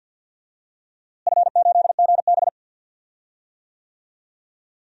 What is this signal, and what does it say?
Morse code, heard as a keyed tone, about 53 wpm, 700 Hz. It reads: F8GB
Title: DXCC-100 Call Signs - 4 Characters - Repeated 3x - 50wpm